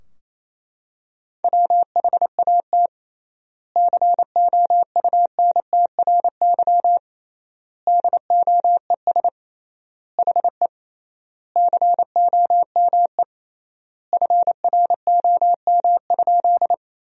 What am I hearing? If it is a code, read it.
Morse code, heard as a keyed tone, about 28 wpm, 700 Hz. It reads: WHAT COUNTRY DOES HE COME FROM?
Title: 3x Word-Spacing - Sentences from Top 300 Words - 28wpm